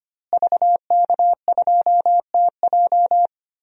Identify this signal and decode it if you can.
Morse code, heard as a keyed tone, about 25 wpm, 700 Hz. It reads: VK2TJ
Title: DXCC-100 Call Signs - 5 Characters - Repeated 2x - 25wpm